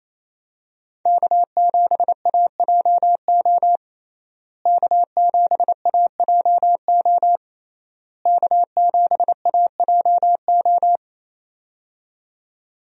Morse code, heard as a keyed tone, about 28 wpm, 700 Hz. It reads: K7AJO K7AJO K7AJO
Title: USA Call Signs - 5 Characters - Repeated 3x - 28wpm